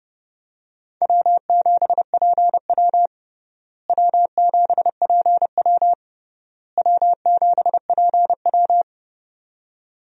Morse code, heard as a keyed tone, about 32 wpm, 700 Hz. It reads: W7PW W7PW W7PW
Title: USA Call Signs - 4 Characters - Repeated 3x - 30wpm